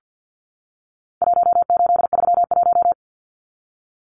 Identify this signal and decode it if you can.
Morse code, heard as a keyed tone, about 50 wpm, 700 Hz. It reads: 1831
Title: Numbers - Contest Style - 4-Digit 50wpm